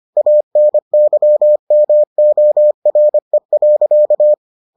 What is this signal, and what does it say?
Morse code, heard as a keyed tone, about 25 wpm, 600 Hz. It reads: ANYMORE.